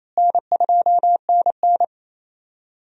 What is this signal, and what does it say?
Morse code, heard as a keyed tone, about 28 wpm, 700 Hz. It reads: N2NN
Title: USA Call Signs - 4 Characters - Repeated 3x - 28wpm